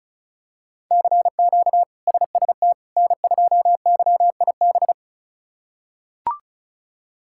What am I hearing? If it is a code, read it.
Morse code, heard as a keyed tone, about 34 wpm, 700 Hz. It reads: CQ SST N2YIB
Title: K1USN SST QSO Practice - 35wpm